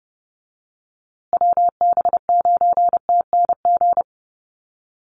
Morse code, heard as a keyed tone, about 30 wpm, 700 Hz. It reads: WB9TNG
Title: USA Call Signs Repeated 3x 30wpm